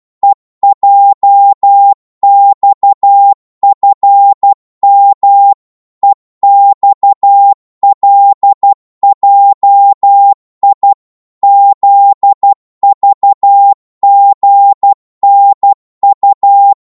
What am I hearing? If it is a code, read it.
Morse code, heard as a keyed tone, about 12 wpm, 800 Hz. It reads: EJXFM EXLJI ZVGNU
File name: Morse / ltr-12wpm